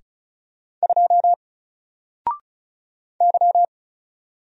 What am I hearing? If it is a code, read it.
Morse code, heard as a keyed tone, about 34 wpm, 700 Hz. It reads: 2 Y
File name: Single-Letter-Number-35wpm